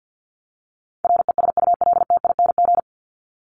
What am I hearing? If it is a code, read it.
Morse code, heard as a keyed tone, about 50 wpm, 700 Hz. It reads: RESULTING
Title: ING Suffix - Top 100 Words with Base Words 50wpm